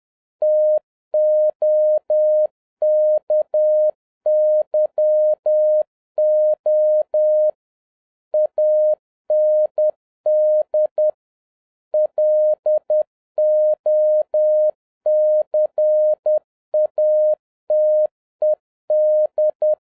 Morse code, heard as a keyed tone, about 10 wpm, 610 Hz. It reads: TOKYO AND LOCATED